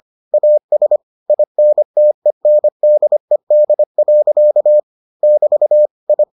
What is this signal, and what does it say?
Morse code, heard as a keyed tone, about 25 wpm, 600 Hz. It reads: AS INTENDED. <BT> I